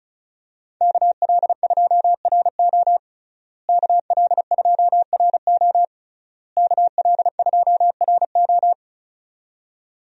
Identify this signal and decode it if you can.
Morse code, heard as a keyed tone, about 36 wpm, 700 Hz. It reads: KL2RO KL2RO KL2RO
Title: USA Call Signs - 5 Characters - Repeated 3x - 35wpm